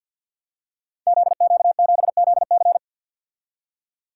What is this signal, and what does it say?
Morse code, heard as a keyed tone, about 47 wpm, 700 Hz. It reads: ZQ7ZC
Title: Code Groups - 5 Letter:Numbers - 50wpm